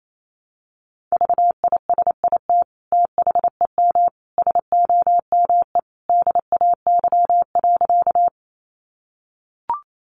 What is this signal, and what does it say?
Morse code, heard as a keyed tone, about 28 wpm, 700 Hz. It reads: VISIT THEM SOME DAY.
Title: Sentences from Top 900 Words 28wpm